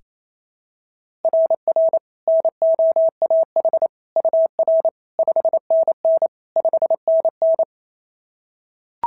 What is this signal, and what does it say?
Morse code, heard as a keyed tone, about 25 wpm, 650 Hz. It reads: RR NOAH UR 5NN 5NN
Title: SOTA QSO Practice 28wpm